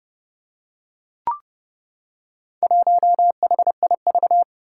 Morse code, heard as a keyed tone, about 30 wpm, 700 Hz. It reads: 1HIV
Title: Four Challenging Letter-Number Combinations 30wpm